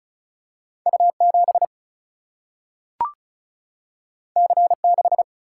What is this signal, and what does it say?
Morse code, heard as a keyed tone, about 36 wpm, 700 Hz. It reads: U 7 C 6
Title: Single-Letter Single-Number 35wpm